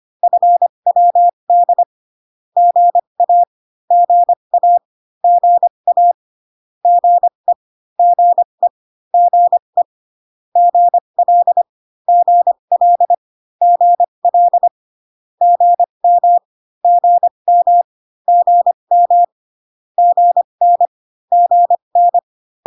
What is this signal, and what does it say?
Morse code, heard as a keyed tone, about 25 wpm, spaced as for 22 wpm, 700 Hz. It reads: FWD GA GA GA GE GE GE GL GL GL GM GM GM GN GN
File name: CW common abbreviations alphabetical repeated 3 times 25 18wpm